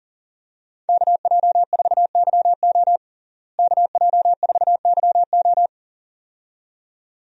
Morse code, heard as a keyed tone, about 40 wpm, 700 Hz. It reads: KJ4YO KJ4YO
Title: USA Call Signs - 5 Characters - Repeated 2x - 40wpm